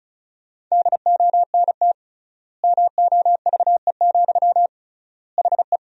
Morse code, heard as a keyed tone, about 36 wpm, 700 Hz. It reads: DONT MOVE, HE
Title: 3x Word-Spacing - Sentences from Top 300 Words - 35wpm